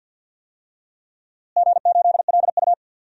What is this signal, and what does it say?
Morse code, heard as a keyed tone, about 53 wpm, 700 Hz. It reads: G8LU
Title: DXCC-100 Call Signs - 4 Characters - Repeated 3x - 50wpm